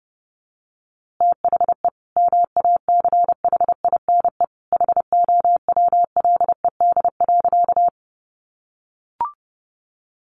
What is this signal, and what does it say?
Morse code, heard as a keyed tone, about 30 wpm, 700 Hz. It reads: THE MACHINE HOWLED.
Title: A Sound Of Thunder 30wpm